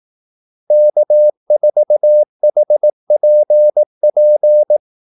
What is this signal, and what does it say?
Morse code, heard as a keyed tone, about 18 wpm, 600 Hz. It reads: K4HPP
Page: Kaldesignaler 18 wpm